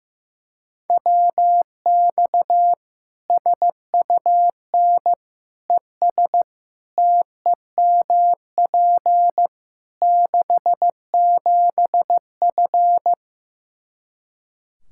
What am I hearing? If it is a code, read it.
Morse code, heard as a keyed tone, about 16 wpm, 700 Hz. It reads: WX SUN ES TEMP 67F
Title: Practice QSO - Exchange 2 - Info - 15wpm